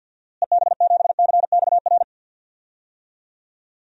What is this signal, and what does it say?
Morse code, heard as a keyed tone, about 52 wpm, 700 Hz. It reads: EB7CXR
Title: DXCC-100 Call Signs - 6 Characters - Repeated 3x - 50wpm